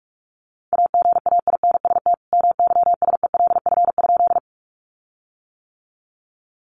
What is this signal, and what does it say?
Morse code, heard as a keyed tone, about 44 wpm, 700 Hz. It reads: AGAINST MYSELF?
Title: Sentences from Top 1300 Words 45wpm